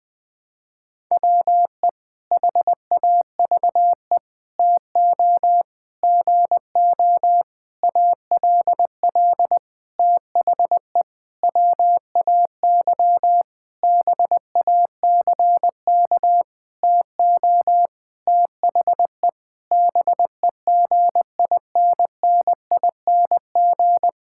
Morse code, heard as a keyed tone, about 20 wpm, 700 Hz. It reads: WE HAVE TO GO ALL THE WAY BACK TO THE BEGINNING
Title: Sentences with 8-Word Phrases - 20wpm